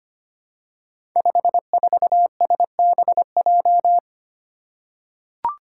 Morse code, heard as a keyed tone, about 25 wpm, 700 Hz. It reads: 54SBJ
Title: Five Challenging Letter-Number Combinations 25wpm